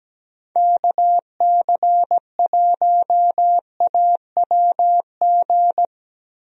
Morse code, heard as a keyed tone, about 17 wpm, 700 Hz. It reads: KC1AWG
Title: USA Call Signs - 6 Characters - Repeated 3x - 17wpm